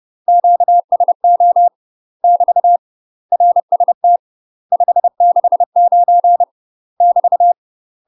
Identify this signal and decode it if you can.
Morse code, heard as a keyed tone, about 30 wpm, 700 Hz. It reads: QSO <BT> RST 569 <BT>
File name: UK MASTER QSOs TRAINING / 30WPM / UK MASTER QSOs 0033